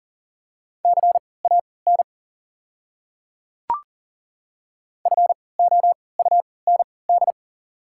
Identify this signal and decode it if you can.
Morse code, heard as a keyed tone, about 40 wpm, spaced as for 23 wpm, 690 Hz. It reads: CAN FOUND
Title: 3x Character Spacing - Top 300 Words - 40wpm